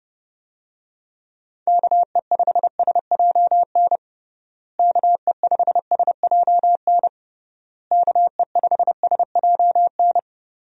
Morse code, heard as a keyed tone, about 30 wpm, 700 Hz. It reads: KE5SJN KE5SJN KE5SJN
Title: USA Call Signs - 6 Characters - Repeated 3x - 30wpm